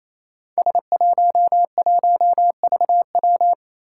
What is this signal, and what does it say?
Morse code, heard as a keyed tone, about 28 wpm, 700 Hz. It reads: S11VW
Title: Five Challenging Letter-Number Combinations 28wpm